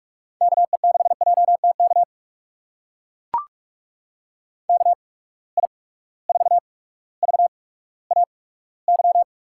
Morse code, heard as a keyed tone, about 43 wpm, spaced as for 19 wpm, 700 Hz. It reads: KE6JTK KI4VAY
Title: USA Call Signs - 6 Characters - 8x Character Spacing - 45wpm